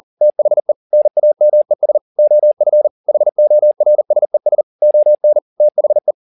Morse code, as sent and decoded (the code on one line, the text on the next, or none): - .... / -. .- -- . ... / --- ..-. / .... --- .-. ... ... / --- -. / - .... .
THE NAMES OF HORSES ON THE